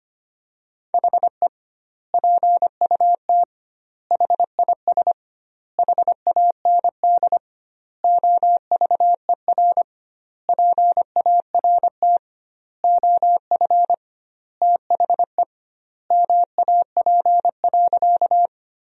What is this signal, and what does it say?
Morse code, heard as a keyed tone, about 25 wpm, 700 Hz. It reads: HE PUT HIS HAND OVER PART OF THE MAP.
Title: 2x Word-Spacing - Sentences from Top 400 Words - 25wpm